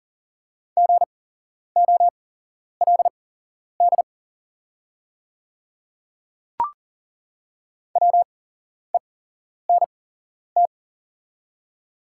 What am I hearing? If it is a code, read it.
Morse code, heard as a keyed tone, about 40 wpm, spaced as for 11 wpm, 700 Hz. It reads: GOLD WENT
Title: 8x Character Spacing - Top 500 Words - 40wpm